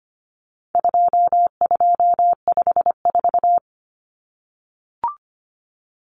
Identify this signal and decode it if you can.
Morse code, heard as a keyed tone, about 25 wpm, 700 Hz. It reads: 2254
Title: Common Cut Numbers - 4-Digit 25wpm